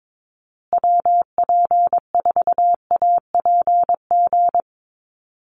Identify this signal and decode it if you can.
Morse code, heard as a keyed tone, about 22 wpm, 700 Hz. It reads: WP4APG
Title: USA Call Signs Repeated 3x 22wpm